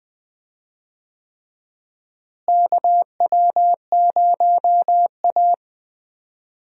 Morse code, heard as a keyed tone, about 20 wpm, 700 Hz. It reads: KW0A
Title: USA Call Signs - 4 Characters - Repeated 3x - 20wpm